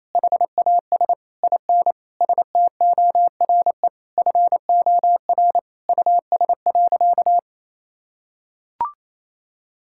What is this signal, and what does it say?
Morse code, heard as a keyed tone, about 27 wpm, 700 Hz. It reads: HAS IN STORE FOR US.
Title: Sentences from Top 1000 Words - Review Part 5 - 28wpm